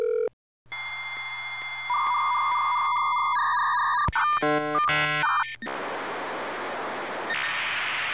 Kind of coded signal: dial-up modem handshake